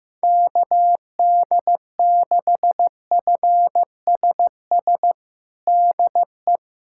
Morse code, heard as a keyed tone, about 15 wpm, 700 Hz. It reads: KD6FSS DE
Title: USA Callsign DE Callsign 15wpm